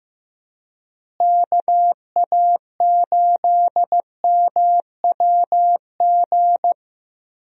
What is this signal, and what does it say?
Morse code, heard as a keyed tone, about 15 wpm, 700 Hz. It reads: KA8MWG
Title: USA Call Signs - 6 Characters - Repeated 2x - 15wpm